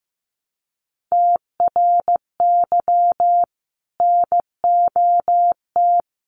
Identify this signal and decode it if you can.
Morse code, heard as a keyed tone, about 15 wpm, 700 Hz. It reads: TRY NOT
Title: Sentences from Top 800 Words 15wpm